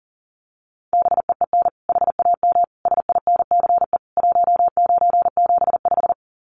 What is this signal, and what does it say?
Morse code, heard as a keyed tone, about 40 wpm, 700 Hz. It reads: BEEN HAM SINCE 1975
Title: Common QSO Phrases 40wpm